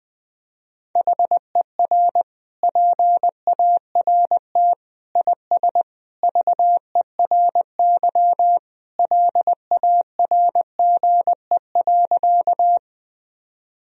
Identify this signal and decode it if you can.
Morse code, heard as a keyed tone, about 20 wpm, 700 Hz. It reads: HER PART IS VERY LARGE.
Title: Sentences from Top 1000 Words - Review Part 1 - 20wpm